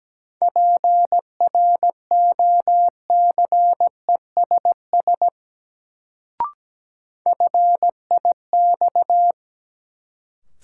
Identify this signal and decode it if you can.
Morse code, heard as a keyed tone, about 17 wpm, 700 Hz. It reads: PROCESS FIX
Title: Sentences from Top 1400 Words - New Word Drill Alt - 17wpm